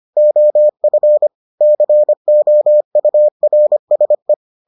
Morse code, heard as a keyed tone, about 25 wpm, 600 Hz. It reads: OF COURSE